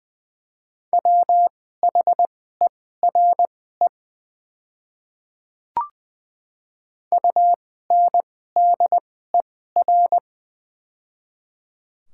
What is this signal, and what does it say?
Morse code, heard as a keyed tone, about 20 wpm, spaced as for 14 wpm, 700 Hz. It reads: WHERE UNDER
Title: 2x Character Spacing - Top 200 Words - 20wpm